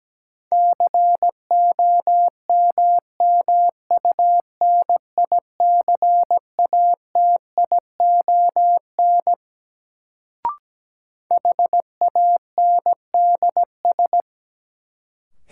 Morse code, heard as a keyed tone, about 17 wpm, 700 Hz. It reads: COMMUNICATION HANDS
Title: Sentences from Top 1000 Words - Review Part 6 - New Word Drill - 17wpm